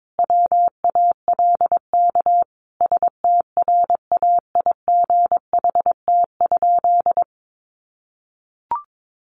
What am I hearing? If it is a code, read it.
Morse code, heard as a keyed tone, about 22 wpm, 700 Hz. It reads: WALK STRAIGHT?
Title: Sentences from Top 1000 Words 22wpm